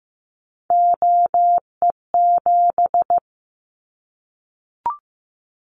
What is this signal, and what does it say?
Morse code, heard as a keyed tone, about 15 wpm, 700 Hz. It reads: OE7
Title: Callsign Prefixes for Top 100 DXCC - 15wpm